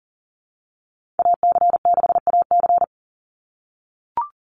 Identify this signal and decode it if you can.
Morse code, heard as a keyed tone, about 40 wpm, 700 Hz. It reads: AC6AC
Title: Call Signs - 40wpm